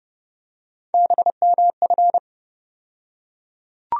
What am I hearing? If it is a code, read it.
Morse code, heard as a keyed tone, about 31 wpm, 700 Hz. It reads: BMF
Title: USA Call Sign Postfixes - 3 Letters 30wpm